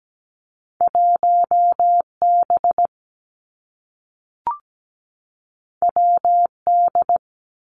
Morse code, heard as a keyed tone, about 17 wpm, 700 Hz. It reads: 1B WD
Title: Two Challenging Letter-Number Combinations 17wpm